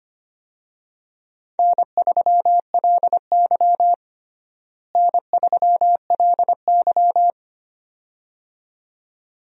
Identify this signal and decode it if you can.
Morse code, heard as a keyed tone, about 25 wpm, 700 Hz. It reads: N3LY N3LY
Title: USA Call Signs - 4 Characters - Repeated 2x - 25wpm